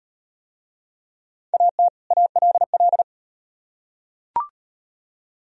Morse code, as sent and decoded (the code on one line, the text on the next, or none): .- - / .- .-.. .-..
AT ALL